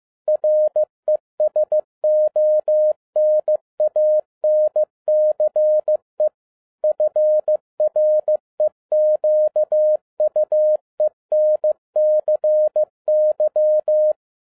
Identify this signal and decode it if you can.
Morse code, heard as a keyed tone, about 15 wpm, 610 Hz. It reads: RESONANCE FREQUENCY
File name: 2012-01-mp3 / n07-ARA JR0DIL-15wpm